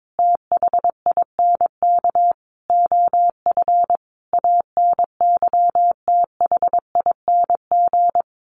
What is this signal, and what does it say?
Morse code, heard as a keyed tone, about 22 wpm, 700 Hz. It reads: THINK OF ANYTHING
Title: Sentences from Top 1300 Words 22wpm